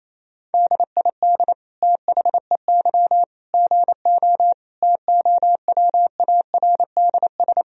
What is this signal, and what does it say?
Morse code, heard as a keyed tone, about 28 wpm, 700 Hz. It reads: DID THEY GO TOWARDS
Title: Sentences from Top 1000 Words - Review Part 2 - 28wpm